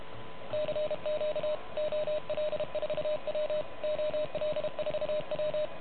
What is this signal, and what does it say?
Morse code, heard as a keyed tone, about 32 wpm, 600 Hz, with noise. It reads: CQ OL4W OL4W
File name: truSDX / MWC29.8